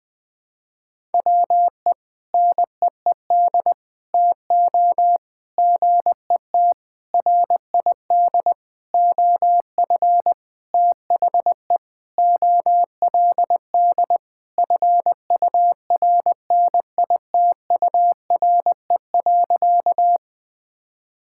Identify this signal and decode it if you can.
Morse code, heard as a keyed tone, about 20 wpm, 690 Hz. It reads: WE NEED TO GET RID OF THE OLD FURNITURE.
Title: Sentences with 4-Word Phrases - Spoken Beginning - 20wpm